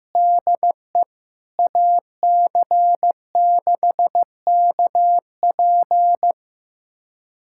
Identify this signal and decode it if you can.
Morse code, heard as a keyed tone, about 15 wpm, 700 Hz. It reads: DE AC6KP
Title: USA Callsign DE Callsign - 5 Characters - 15wpm